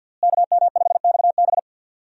Morse code, heard as a keyed tone, about 51 wpm, 690 Hz. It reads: KM5XB
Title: USA Call Signs - 5 Characters - Repeated 3x - 50wpm